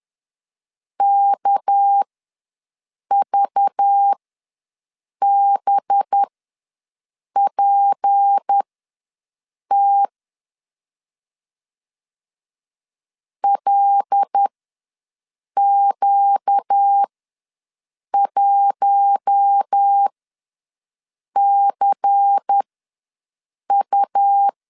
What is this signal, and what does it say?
Morse code, heard as a keyed tone, about 10 wpm, spaced as for 6 wpm, 800 Hz. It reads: KVBPT LQ1CU
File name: LEZIONI / 6 PKX0 / 6 PKX0 ws 10wpm 4220